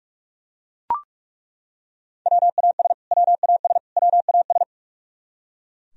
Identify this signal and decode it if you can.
Morse code, heard as a keyed tone, about 42 wpm, 700 Hz. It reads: WAS WAS WAS
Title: Top 100 Words Repeated 45wpm